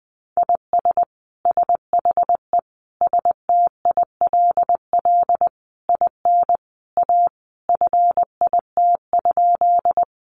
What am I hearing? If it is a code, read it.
Morse code, heard as a keyed tone, about 20 wpm, 700 Hz. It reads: IS SHE STILL IN A FIT?
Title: Sentences from Top 800 Words 20wpm